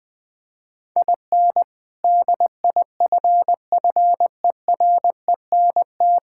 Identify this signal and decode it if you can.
Morse code, heard as a keyed tone, about 20 wpm, 700 Hz. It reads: IN DIFFERENT
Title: Sentences with 3-Word Phrases - 20wpm